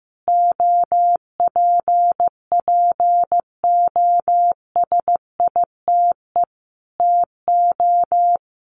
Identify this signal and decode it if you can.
Morse code, heard as a keyed tone, about 15 wpm, 700 Hz. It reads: OPPOSITE TO
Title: Sentences from Top 1300 Words 15wpm